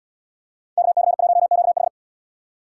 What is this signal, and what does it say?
Morse code, heard as a keyed tone, about 73 wpm, 700 Hz. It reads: KC1QH